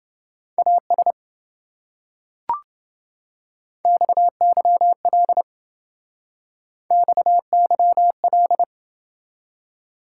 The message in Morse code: AS XYL XYL

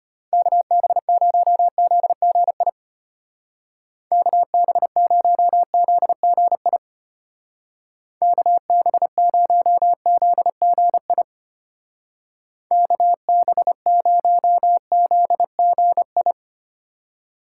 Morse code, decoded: KB0ZGI KB0ZGI KB0ZGI KB0ZGI

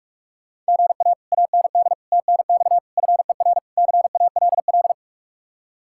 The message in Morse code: GA AND TNX FER CALL

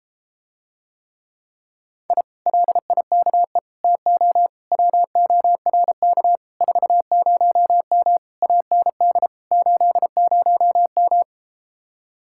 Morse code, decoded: I LIKE TO WORK 40M AND 80M

